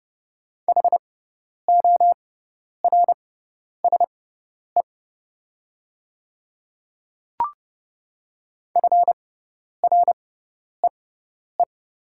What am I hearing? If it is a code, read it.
Morse code, heard as a keyed tone, about 30 wpm, spaced as for 10 wpm, 690 Hz. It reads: HORSE FREE
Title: 6x Character Spacing - Top 500 Words - 30wpm